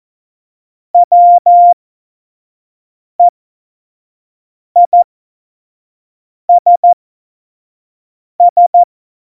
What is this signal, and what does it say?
Morse code, heard as a keyed tone, about 14 wpm, spaced as for 5 wpm, 700 Hz. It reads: WEISS
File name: Morsekurs / Witze / Witz 28. August 2018, 25 F70